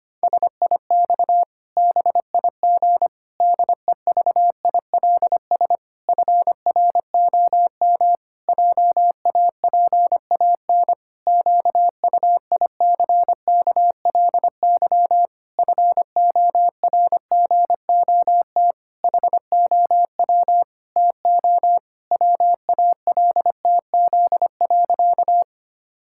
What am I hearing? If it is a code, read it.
Morse code, heard as a keyed tone, about 25 wpm, 700 Hz. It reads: SIX BIG DEVILS FROM JAPAN QUICKLY FORGOT HOW TO WALTZ.
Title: Pangram Challenge v2 - 25wpm